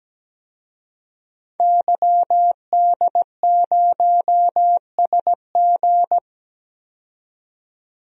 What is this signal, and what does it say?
Morse code, heard as a keyed tone, about 17 wpm, 700 Hz. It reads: YD0SG
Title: DXCC-100 Call Signs - 5 Characters - 17wpm